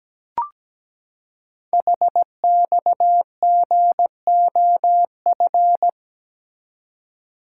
Morse code, heard as a keyed tone, about 17 wpm, 700 Hz. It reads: HXGOF
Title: Code Groups - 5 Letters - 17wpm